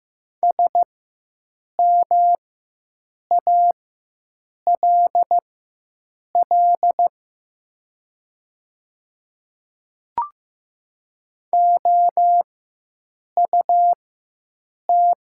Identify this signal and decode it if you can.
Morse code, heard as a keyed tone, about 15 wpm, spaced as for 7 wpm, 700 Hz. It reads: SMALL OUT
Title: 4x Character Spacing - Top 200 Words - 15wpm